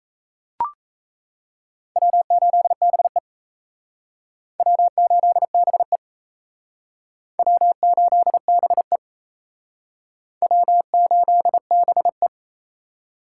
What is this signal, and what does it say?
Morse code, heard as a keyed tone, about 35 wpm, 700 Hz. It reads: W8BE W8BE W8BE W8BE
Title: Speed Racing - USA Call Signs 28wpm